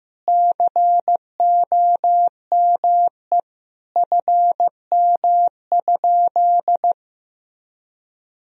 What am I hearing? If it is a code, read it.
Morse code, heard as a keyed tone, about 15 wpm, 700 Hz. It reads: COME FM?
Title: Sentences from Top 300 Words and Abbreviations 15wpm